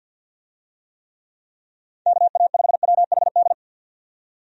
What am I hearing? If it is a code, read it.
Morse code, heard as a keyed tone, about 52 wpm, 700 Hz. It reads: KA5WHD
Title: USA Call Signs - 6 Characters - 50wpm